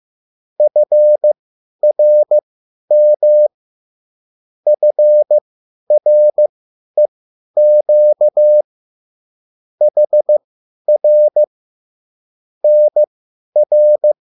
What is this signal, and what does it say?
Morse code, heard as a keyed tone, about 15 wpm, spaced as for 10 wpm, 600 Hz. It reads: FRM FREQ HR NR
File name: QSL / CW lesson